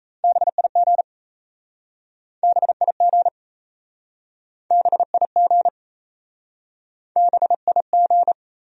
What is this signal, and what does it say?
Morse code, heard as a keyed tone, about 37 wpm, 700 Hz. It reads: BIG BIG BIG BIG